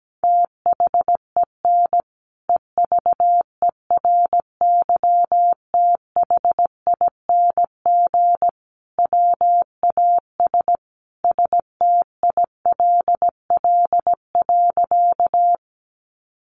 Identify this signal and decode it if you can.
Morse code, heard as a keyed tone, about 17 wpm, 700 Hz. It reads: THEN EVERYTHING WAS STILL.
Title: Sentences from Top 700 Words 17wpm